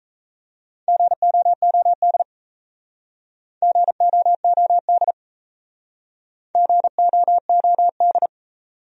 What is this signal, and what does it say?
Morse code, heard as a keyed tone, about 36 wpm, 700 Hz. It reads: GOOD GOOD GOOD